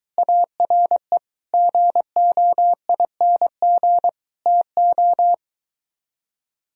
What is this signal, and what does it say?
Morse code, heard as a keyed tone, about 23 wpm, 700 Hz. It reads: ARE GOING TO